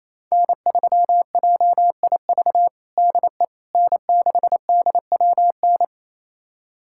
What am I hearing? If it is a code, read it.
Morse code, heard as a keyed tone, about 28 wpm, 700 Hz. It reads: N3JIV DE N6DWN
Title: USA Callsign DE Callsign - 5 Characters - 28wpm